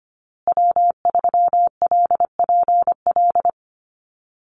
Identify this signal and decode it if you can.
Morse code, heard as a keyed tone, about 25 wpm, 700 Hz. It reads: W3LPL
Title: Call Signs Repeated 3x - 25wpm